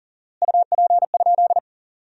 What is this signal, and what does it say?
Morse code, heard as a keyed tone, about 40 wpm, 700 Hz. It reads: UP?
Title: 4x Word-Spacing - Sentences from Top 500 Words - 40wpm